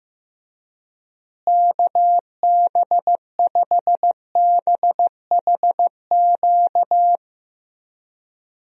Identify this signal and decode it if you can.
Morse code, heard as a keyed tone, about 15 wpm, 700 Hz. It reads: KB5BHQ
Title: USA Call Signs - 6 Characters - Repeated 3x - 15wpm